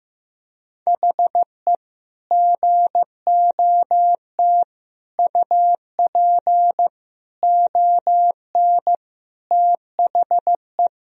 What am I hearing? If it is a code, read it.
Morse code, heard as a keyed tone, about 16 wpm, 700 Hz. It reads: HE GOT UP ON THE
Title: Sentences from Top 1000 Words - Review Part 1 - 15wpm